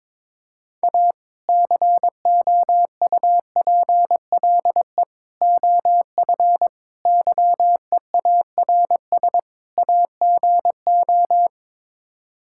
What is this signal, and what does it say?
Morse code, heard as a keyed tone, about 22 wpm, 700 Hz. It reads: A COUPLE OF YEARS AGO
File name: practice / 22wpm / Build A Phrase - 5-Word Phrases - 22wpm